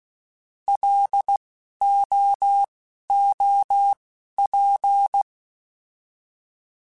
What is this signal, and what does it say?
Morse code, heard as a keyed tone, about 16 wpm, spaced as for 12 wpm, 800 Hz. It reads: LOOP